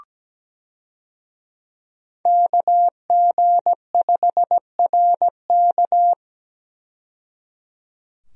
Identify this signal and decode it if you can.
Morse code, heard as a keyed tone, about 17 wpm, 700 Hz. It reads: KG5RK
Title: USA Call Signs - 5 Characters - 17wpm